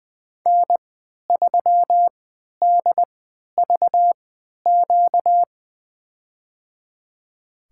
Morse code, heard as a keyed tone, about 20 wpm, spaced as for 11 wpm, 700 Hz. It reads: N3DVQ
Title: USA Call Signs - 5 Characters - 3x Character Spacing - 20wpm